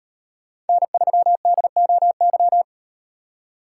Morse code, heard as a keyed tone, about 38 wpm, 700 Hz. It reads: N3DOY